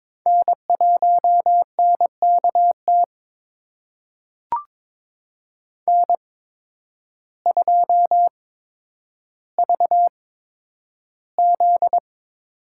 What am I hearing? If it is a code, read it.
Morse code, heard as a keyed tone, about 22 wpm, spaced as for 10 wpm, 700 Hz. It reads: N1NKT N2VZ
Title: USA Call Signs - 5 Characters - 8x Character Spacing - 22wpm